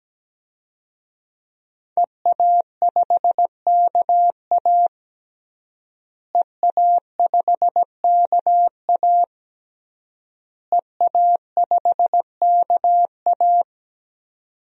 Morse code, heard as a keyed tone, about 17 wpm, 700 Hz. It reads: EA5KA EA5KA EA5KA
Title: DXCC-100 Call Signs - 5 Characters - Repeated 3x - 17wpm